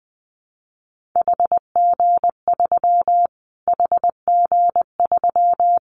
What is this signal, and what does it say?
Morse code, heard as a keyed tone, about 19 wpm, 700 Hz. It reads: HG3 HG3
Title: Callsign Prefixes for Top 25 DXCC - Repeated 3x - 20wpm